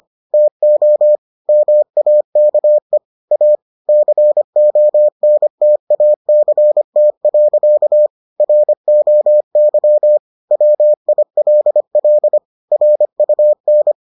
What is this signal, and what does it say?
Morse code, heard as a keyed tone, about 25 wpm, 600 Hz. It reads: TO MAKE A CONTACT. ROY WILL RUN